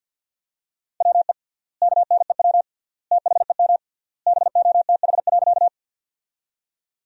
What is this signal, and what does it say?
Morse code, heard as a keyed tone, about 50 wpm, 700 Hz. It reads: WE KNEW THEM BOTH.
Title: 3x Word-Spacing - Sentences from Top 400 Words - 50wpm